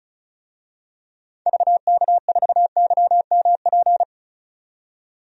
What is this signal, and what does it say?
Morse code, heard as a keyed tone, about 34 wpm, 700 Hz. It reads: VK4YMP
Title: DXCC-100 Call Signs - 6 Characters - 35wpm